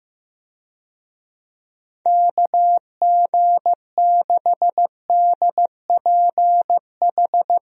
Morse code, heard as a keyed tone, about 15 wpm, 700 Hz. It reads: KG6DPH
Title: USA Call Signs - 6 Characters - 15wpm